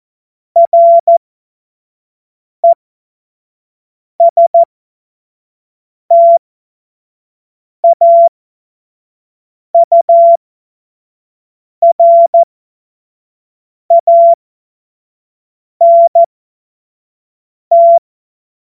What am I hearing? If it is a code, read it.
Morse code, heard as a keyed tone, about 14 wpm, spaced as for 5 wpm, 680 Hz. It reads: RESTAURANT
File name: Morsekurs / Witze / Witz 29. Juni 2021, 25 F70